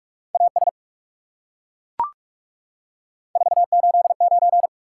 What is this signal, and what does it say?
Morse code, heard as a keyed tone, about 36 wpm, 700 Hz. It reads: AS 489
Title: Speed-Racing Top 180 QSO Elements 30wpm